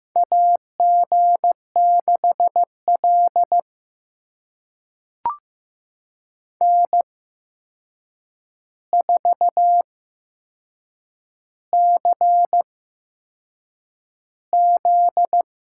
Morse code, heard as a keyed tone, about 15 wpm, spaced as for 6 wpm, 700 Hz. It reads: AG6L N4CZ
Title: USA Call Signs - 4 Characters - 8x Character Spacing - 15wpm